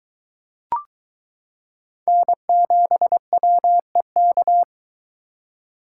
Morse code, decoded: N7WEK

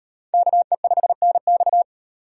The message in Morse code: KE5NX

